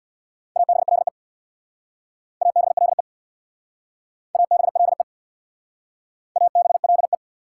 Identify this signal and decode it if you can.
Morse code, heard as a keyed tone, about 62 wpm, 700 Hz. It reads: ABLE ABLE ABLE ABLE
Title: Speed-Racing Prefixes and Suffixes 50wpm